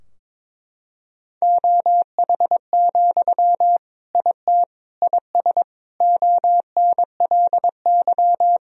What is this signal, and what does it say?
Morse code, heard as a keyed tone, about 22 wpm, 700 Hz. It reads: OH, IT IS ONLY
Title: Sentences from Top 1000 Words - Review Part 3 - 22wpm